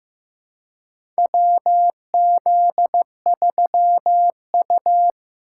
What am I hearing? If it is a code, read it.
Morse code, heard as a keyed tone, about 15 wpm, 700 Hz. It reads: WZ3U
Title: USA Call Signs - 2x Character Spacing - 15wpm